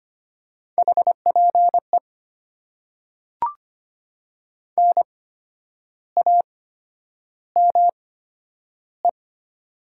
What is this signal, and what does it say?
Morse code, heard as a keyed tone, about 25 wpm, spaced as for 10 wpm, 700 Hz. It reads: HPE NAME
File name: Top 90 QSO Elements - 8x Character Spacing - 25wpm